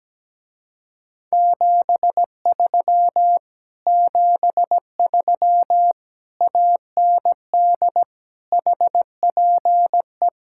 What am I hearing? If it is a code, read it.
Morse code, heard as a keyed tone, about 17 wpm, 700 Hz. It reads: 73 73 AND HPE
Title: Speed-Racing Common QSO Phrases - 17wpm